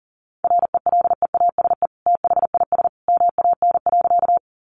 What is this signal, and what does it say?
Morse code, heard as a keyed tone, about 40 wpm, 700 Hz. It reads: RELEASE THIS MAN.
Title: Sentences from Top 1100 Words 40wpm